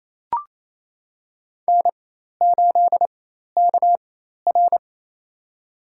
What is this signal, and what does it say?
Morse code, heard as a keyed tone, about 28 wpm, spaced as for 13 wpm, 700 Hz. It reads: N8KR
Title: USA Call Signs - 4x Character Spacing - 28wpm